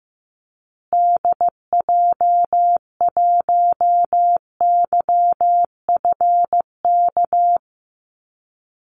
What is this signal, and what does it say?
Morse code, heard as a keyed tone, about 14 wpm, 700 Hz. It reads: DJ1YFK
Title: Call Signs Repeated 2x - 15wpm